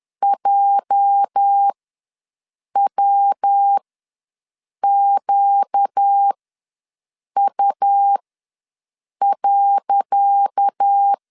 Morse code, heard as a keyed tone, about 10 wpm, spaced as for 6 wpm, 800 Hz. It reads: JWQU.
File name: LEZIONI / 10 SIGNS / 10C DOT ws 10wpm 4213